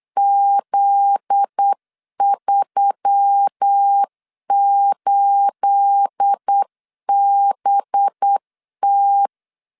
Morse code, heard as a keyed tone, about 8 wpm, 800 Hz. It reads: Z38BT
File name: LEZIONI / 9 28 8wpm 3512